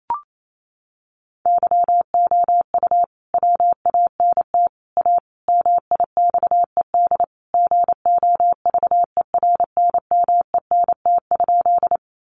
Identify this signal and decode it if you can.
Morse code, heard as a keyed tone, about 28 wpm, 700 Hz. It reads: YOU WANT A MIXED GOVERNMENT?
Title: Sentences from Top 1500 Words 28wpm